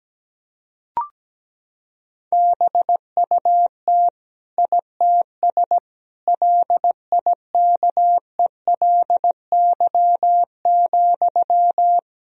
Morse code, heard as a keyed tone, about 17 wpm, 700 Hz. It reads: BUT ITS LIKELY,
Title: Sentences from Top 1000 Words - Review Part 4 - 17wpm